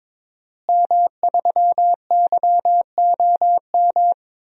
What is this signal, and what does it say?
Morse code, heard as a keyed tone, about 22 wpm, 700 Hz. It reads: M3YOM
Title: DXCC-100 Call Signs - 5 Characters - Repeated 3x - 22wpm